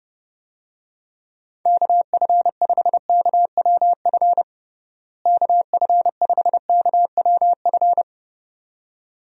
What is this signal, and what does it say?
Morse code, heard as a keyed tone, about 32 wpm, 700 Hz. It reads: KF5KWF KF5KWF
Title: USA Call Signs - 6 Characters - Repeated 2x - 30wpm